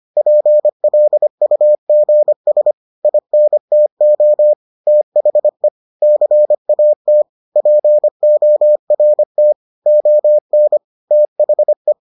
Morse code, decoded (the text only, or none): PLUGS INTO THE CAT PORT ON THE